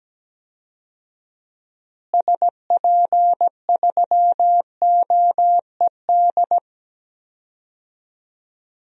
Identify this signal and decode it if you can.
Morse code, heard as a keyed tone, about 17 wpm, 700 Hz. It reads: SP3OED
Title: DXCC-100 Call Signs - 6 Characters - Repeated 3x - 17wpm